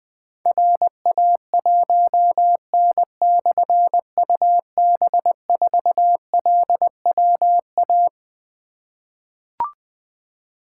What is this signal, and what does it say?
Morse code, heard as a keyed tone, about 21 wpm, 700 Hz. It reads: RA1N/UB4LWA
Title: Difficult Call Signs 20wpm